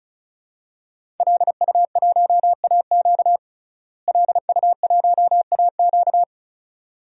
Morse code, heard as a keyed tone, about 36 wpm, 700 Hz. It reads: LU1AQ LU1AQ
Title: DXCC-100 Call Signs - 5 Characters - Repeated 2x - 35wpm